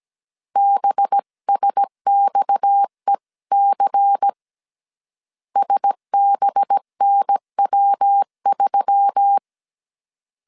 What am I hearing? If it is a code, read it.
Morse code, heard as a keyed tone, about 17 wpm, 800 Hz. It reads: BSXEC SBNW3
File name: CW / DET / 1 6 1500C S A 16wpm 2635